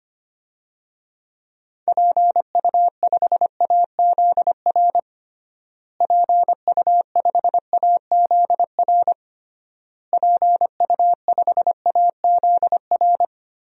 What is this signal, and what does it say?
Morse code, heard as a keyed tone, about 25 wpm, 700 Hz. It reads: PU5AZR PU5AZR PU5AZR
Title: DXCC-100 Call Signs - 6 Characters - Repeated 3x - 25wpm